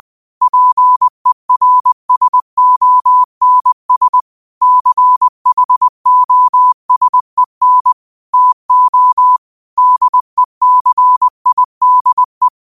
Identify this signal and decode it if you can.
Morse code, heard as a keyed tone, about 20 wpm, 1000 Hz. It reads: PERSONS CHOSEN TO DECIDE
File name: qotdfiles / 20wpm